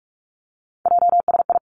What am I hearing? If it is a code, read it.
Morse code, heard as a keyed tone, about 46 wpm, 700 Hz. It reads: JSI
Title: Three Challenging Letter-Number Combinations 45wpm